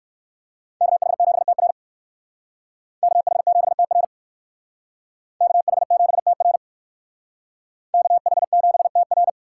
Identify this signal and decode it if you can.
Morse code, heard as a keyed tone, about 59 wpm, 700 Hz. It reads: KH7TR KH7TR KH7TR KH7TR